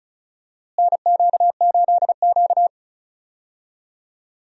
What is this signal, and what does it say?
Morse code, heard as a keyed tone, about 36 wpm, 700 Hz. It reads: NQ8Q
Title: USA Call Signs - 4 Characters - Repeated 3x - 35wpm